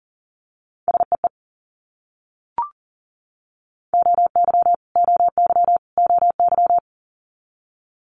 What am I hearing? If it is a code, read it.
Morse code, heard as a keyed tone, about 41 wpm, 700 Hz. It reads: SEE OY OY OY
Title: Multi-letter Phonemes Repeated 3x 40wpm